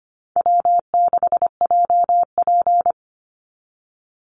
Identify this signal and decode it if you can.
Morse code, heard as a keyed tone, about 25 wpm, 700 Hz. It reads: W6JP
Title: NA SOTA Chaser Callsigns 25wpm